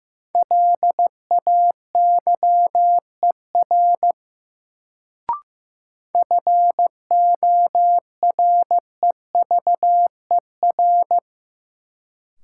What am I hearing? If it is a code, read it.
Morse code, heard as a keyed tone, about 15 wpm, 700 Hz. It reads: LAYER FOREVER
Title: Sentences from Top 1600 Words - New Word Drill Alt - 15wpm